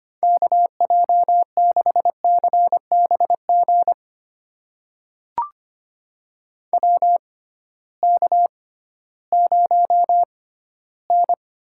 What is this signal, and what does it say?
Morse code, heard as a keyed tone, about 25 wpm, spaced as for 14 wpm, 700 Hz. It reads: KJ6CBG WK0N
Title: USA Call Signs - 6x Character Spacing - 25wpm